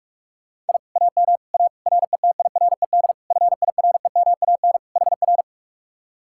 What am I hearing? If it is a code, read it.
Morse code, heard as a keyed tone, about 59 wpm, spaced as for 51 wpm, 700 Hz. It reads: I AM A RETIRED FIREMAN HR
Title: Speed-Racing Common QSO Phrases Encore - 45wpm